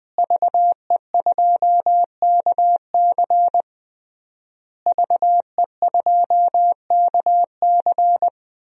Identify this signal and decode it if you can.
Morse code, heard as a keyed tone, about 20 wpm, 700 Hz. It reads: VE2KC VE2KC
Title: DXCC-100 Call Signs - 5 Characters - Repeated 3x - 20wpm